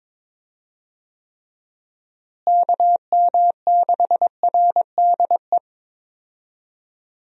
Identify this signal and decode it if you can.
Morse code, heard as a keyed tone, about 22 wpm, 700 Hz. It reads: KM6RDE
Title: USA Call Signs - 6 Characters - Repeated 2x - 22wpm